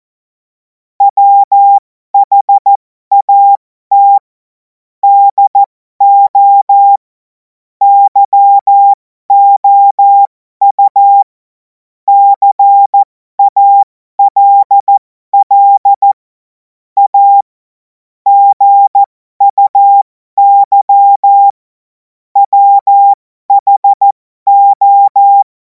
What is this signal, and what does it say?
Morse code, heard as a keyed tone, about 14 wpm, spaced as for 12 wpm, 800 Hz. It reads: WHAT DO YOU CALL A GUY WHO
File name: Morsekurs / Witze